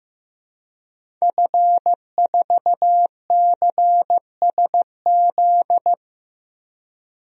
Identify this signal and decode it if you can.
Morse code, heard as a keyed tone, about 15 wpm, 700 Hz. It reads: F4CSZ
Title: DXCC-100 Call Signs - 5 Characters - 15wpm